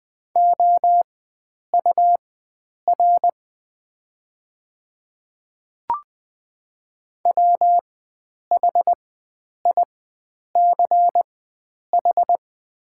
Morse code, heard as a keyed tone, about 20 wpm, spaced as for 9 wpm, 700 Hz. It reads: OUR WHICH
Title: 4x Character Spacing - Top 300 Words - 20wpm